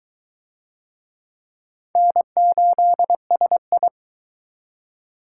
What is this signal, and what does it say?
Morse code, heard as a keyed tone, about 22 wpm, 700 Hz. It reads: N8SI